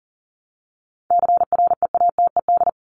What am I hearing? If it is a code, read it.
Morse code, heard as a keyed tone, about 40 wpm, 700 Hz. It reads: CREATED
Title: ED Suffix - Top 100 Words with Base Words 40wpm